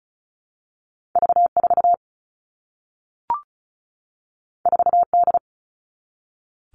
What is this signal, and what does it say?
Morse code, heard as a keyed tone, about 36 wpm, 700 Hz. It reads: V4 4D
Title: All Cut Numbers - 2-Digit 35wpm